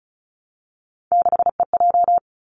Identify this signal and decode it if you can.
Morse code, heard as a keyed tone, about 36 wpm, 700 Hz. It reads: 6EJ
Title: Three Challenging Letter-Number Combinations 35wpm